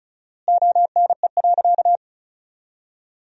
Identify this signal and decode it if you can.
Morse code, heard as a keyed tone, about 36 wpm, 700 Hz. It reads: ONE.
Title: 4x Word-Spacing - Sentences from Top 300 Words - 35wpm